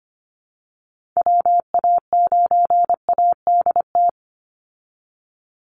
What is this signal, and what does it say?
Morse code, heard as a keyed tone, about 25 wpm, 700 Hz. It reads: WA9ADT
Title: Call Signs Repeated 3x - 25wpm